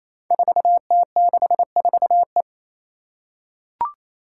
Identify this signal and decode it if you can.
Morse code, heard as a keyed tone, about 27 wpm, 700 Hz. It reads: 4T64E
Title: Five Challenging Letter-Number Combinations 28wpm